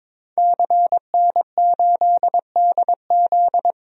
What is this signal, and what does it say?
Morse code, heard as a keyed tone, about 22 wpm, 700 Hz. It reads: CN8DZ
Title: DXCC-100 Call Signs - 5 Characters - 22wpm